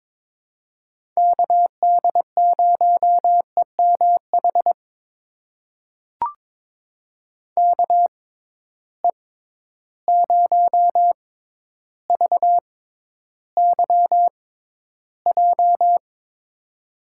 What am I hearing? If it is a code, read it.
Morse code, heard as a keyed tone, about 22 wpm, spaced as for 11 wpm, 700 Hz. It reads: KD0EMH KE0VYJ
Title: USA Call Signs - 6 Characters - 6x Character Spacing - 22wpm